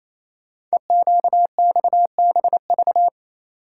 Morse code, heard as a keyed tone, about 27 wpm, 700 Hz. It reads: EQXBV
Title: Code Groups - 5 Letter:Numbers - 28wpm